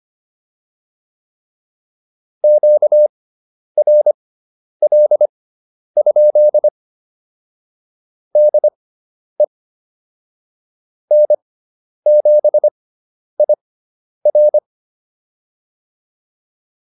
Morse code, heard as a keyed tone, about 25 wpm, spaced as for 10 wpm, 600 Hz. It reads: QRL? DE N7IR